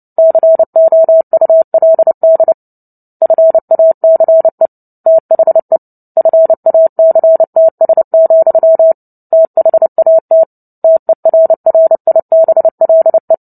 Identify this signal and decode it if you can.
Morse code, heard as a keyed tone, about 29 wpm, 650 Hz. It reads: COULD FACE THE FACTS, THAT TERRIBLE
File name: Book Two / 03-Chapter 3 - 29 WPM